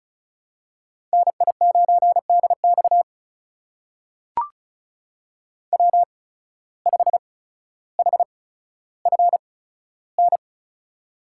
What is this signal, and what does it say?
Morse code, heard as a keyed tone, about 34 wpm, spaced as for 15 wpm, 700 Hz. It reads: NI9DX W5HFN
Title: USA Call Signs - 5 Characters - 8x Character Spacing - 35wpm